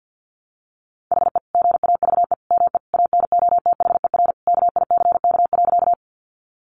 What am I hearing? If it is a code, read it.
Morse code, heard as a keyed tone, about 48 wpm, 700 Hz. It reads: HE GAVE ME ANOTHER KICK.
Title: Sentences from Top 1300 Words 50wpm